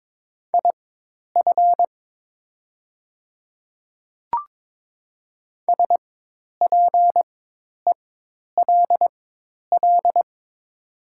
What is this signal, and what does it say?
Morse code, heard as a keyed tone, about 22 wpm, spaced as for 10 wpm, 700 Hz. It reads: IF SPELL
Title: 4x Character Spacing - Top 200 Words - 22wpm